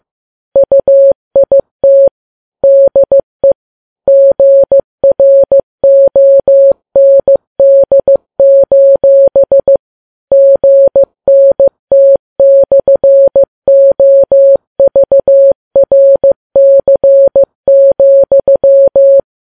Morse code, heard as a keyed tone, about 15 wpm, 550 Hz. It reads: UIT DE GROND: GNT/OVRC,
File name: Files Web tls / Morse on9fv / morse15kps